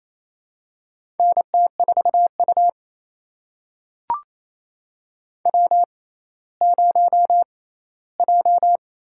Morse code, heard as a keyed tone, about 28 wpm, spaced as for 16 wpm, 700 Hz. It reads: NT4U W0J
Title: USA Call Signs - 6x Character Spacing - 28wpm